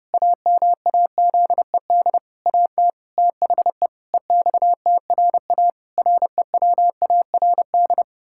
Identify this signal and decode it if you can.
Morse code, heard as a keyed tone, about 30 wpm, 700 Hz. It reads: AMAZED AT THE EXTRA REWARD
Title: Pangram Challenge v2 - 30wpm